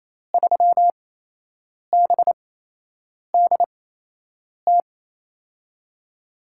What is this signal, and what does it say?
Morse code, heard as a keyed tone, about 28 wpm, spaced as for 8 wpm, 700 Hz. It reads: 3BDT
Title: USA Call Signs - 5 Characters - 8x Character Spacing - 28wpm